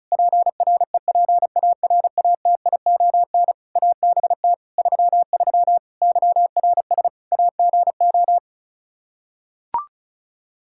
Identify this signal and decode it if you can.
Morse code, heard as a keyed tone, about 36 wpm, 700 Hz. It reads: PREPARATION ABT 33 YRS AGO
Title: K1USN SST QSO Practice 35wpm